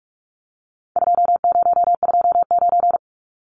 Morse code, heard as a keyed tone, about 44 wpm, 700 Hz. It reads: 2029
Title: Numbers - 4-Digit 45wpm